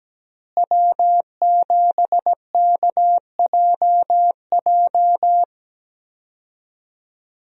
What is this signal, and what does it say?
Morse code, heard as a keyed tone, about 17 wpm, 700 Hz. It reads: W7KJJ
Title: USA Call Signs - 5 Characters - Repeated 2x - 17wpm